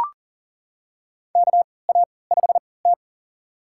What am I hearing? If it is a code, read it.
Morse code, heard as a keyed tone, about 37 wpm, spaced as for 22 wpm, 700 Hz. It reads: KA5T
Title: USA Call Signs - 3x Character Spacing - 40wpm